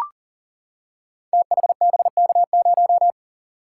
Morse code, heard as a keyed tone, about 39 wpm, 690 Hz. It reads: THBK0
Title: Code Groups - 5 Letter:Numbers - 40wpm